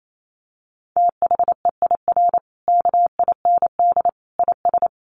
Morse code, heard as a keyed tone, about 28 wpm, 700 Hz. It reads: THEIR KIND IS
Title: Sentences from Top 1500 Words 28wpm